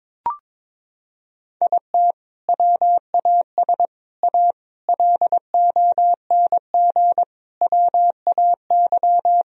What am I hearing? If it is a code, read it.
Morse code, heard as a keyed tone, about 22 wpm, 700 Hz. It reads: IT WAS A LONG WAY
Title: Sentences from Top 1000 Words - Review Part 2 - 22wpm